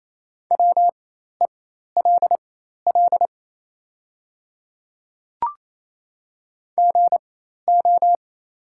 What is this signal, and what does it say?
Morse code, heard as a keyed tone, about 27 wpm, spaced as for 13 wpm, 700 Hz. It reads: WELL GO
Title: 4x Character Spacing - Top 300 Words - 28wpm